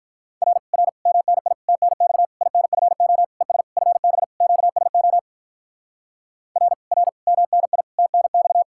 Morse code, heard as a keyed tone, about 48 wpm, 700 Hz. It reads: R R MNI TNX INFO ES FB QSO R R MNI TNX